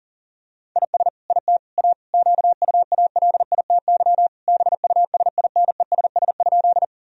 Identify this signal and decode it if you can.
Morse code, heard as a keyed tone, about 40 wpm, 700 Hz. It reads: IS IT A QUALITY BUSINESS?
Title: Sentences from Top 1000 Words - Review Part 4 - 40wpm